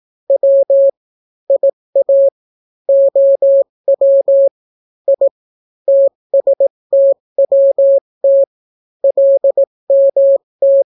Morse code, heard as a keyed tone, about 18 wpm, 550 Hz. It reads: W IA OW I TSTWT LMT